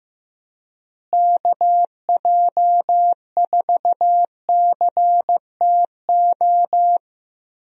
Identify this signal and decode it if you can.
Morse code, heard as a keyed tone, about 15 wpm, 700 Hz. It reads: KJ4CTO
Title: USA Call Signs - 6 Characters - Repeated 2x - 15wpm